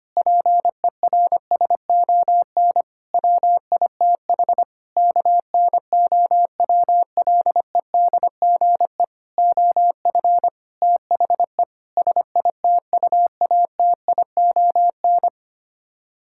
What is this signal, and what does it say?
Morse code, heard as a keyed tone, about 25 wpm, 700 Hz. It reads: PERSON WITH KNOWLEDGE OF THE SITUATION
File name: Sets of 9 Words - 25wpm